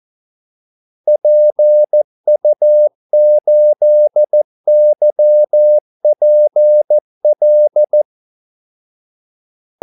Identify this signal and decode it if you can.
Morse code, heard as a keyed tone, about 14 wpm, 600 Hz. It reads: PU8YPL
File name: Callsigns-14wpm